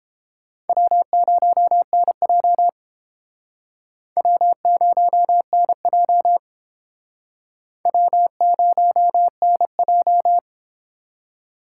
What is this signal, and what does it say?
Morse code, heard as a keyed tone, about 29 wpm, 700 Hz. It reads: W0NJ W0NJ W0NJ